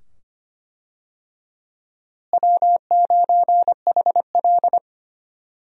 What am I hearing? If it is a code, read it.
Morse code, heard as a keyed tone, about 25 wpm, 700 Hz. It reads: W9HL
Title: USA Call Signs - 4 Characters - Repeated 2x - 25wpm